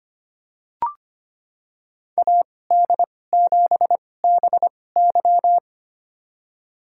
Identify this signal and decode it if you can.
Morse code, heard as a keyed tone, about 25 wpm, spaced as for 18 wpm, 700 Hz. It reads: AD7BY
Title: USA Call Signs - 5 Characters - 2x Character Spacing - 25wpm